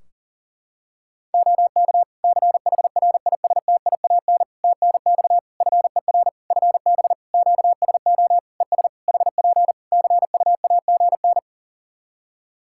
Morse code, heard as a keyed tone, about 40 wpm, 700 Hz. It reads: OK CHRISTIAN TNX FER FB QSO ES HP CUAGN
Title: Practice QSO - Exchange 4 - Ending - 40wpm